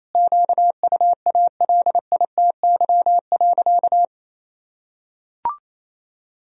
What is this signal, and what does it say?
Morse code, heard as a keyed tone, about 28 wpm, 700 Hz. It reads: QUALITY.
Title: Sentences from Top 1000 Words - Review Part 4 - 28wpm